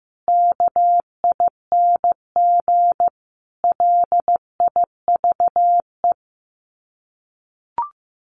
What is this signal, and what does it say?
Morse code, heard as a keyed tone, about 15 wpm, 700 Hz. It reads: KING LIVE
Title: Sets of 5 Words Encore2 15wpm